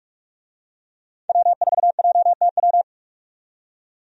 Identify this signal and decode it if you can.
Morse code, heard as a keyed tone, about 46 wpm, 700 Hz. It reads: W4JTW
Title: USA Call Signs - 5 Characters - 45wpm